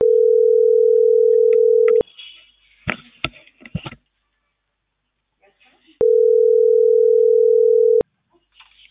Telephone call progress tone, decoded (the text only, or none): ringback tone